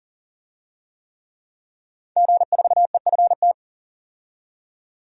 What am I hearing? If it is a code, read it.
Morse code, heard as a keyed tone, about 40 wpm, 700 Hz. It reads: G4EFT
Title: DXCC-100 Call Signs - 5 Characters - 40wpm